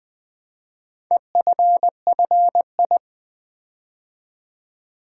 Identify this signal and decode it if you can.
Morse code, heard as a keyed tone, about 20 wpm, 700 Hz. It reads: EFFI
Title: Build A Word - 4 Letters At A Time - 10-Letter Words - 20wpm